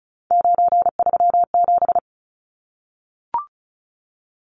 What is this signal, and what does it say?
Morse code, heard as a keyed tone, about 34 wpm, 700 Hz. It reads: 937
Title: Numbers - 3-Digit 35wpm